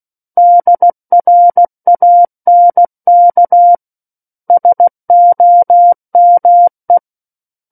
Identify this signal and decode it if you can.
Morse code, heard as a keyed tone, about 16 wpm, 700 Hz. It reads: DRANK SOME
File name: Book One / 07-Chapter 7 - 16 WPM